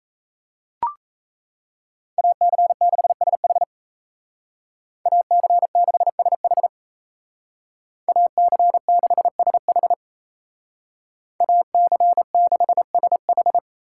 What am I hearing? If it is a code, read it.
Morse code, heard as a keyed tone, about 35 wpm, 700 Hz. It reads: AC6SH AC6SH AC6SH AC6SH
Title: Speed Racing - USA 2x2 Call Signs 28wpm